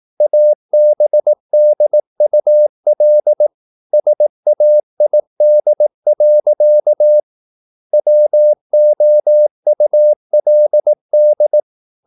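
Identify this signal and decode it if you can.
Morse code, heard as a keyed tone, about 18 wpm, 600 Hz. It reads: ABDUL SAID. WOULD
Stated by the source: Morse practice at 18 wpm, English text, from CW med Gnister.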